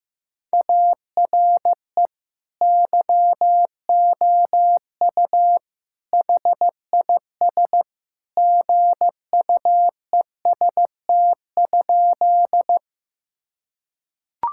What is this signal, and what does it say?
Morse code, heard as a keyed tone, about 16 wpm, 700 Hz. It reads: ARE YOU HIS GUEST?
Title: Sentences from Top 1600 Words 15wpm